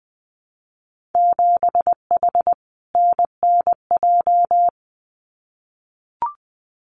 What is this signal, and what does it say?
Morse code, heard as a keyed tone, about 20 wpm, 700 Hz. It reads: 7H NNJ
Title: Winter Field Day Exchanges Encore 20wpm